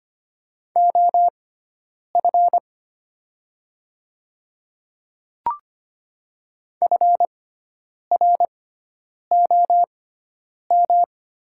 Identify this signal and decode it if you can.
Morse code, heard as a keyed tone, about 25 wpm, spaced as for 9 wpm, 700 Hz. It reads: OF FROM
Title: 6x Character Spacing - Top 100 Words - 25wpm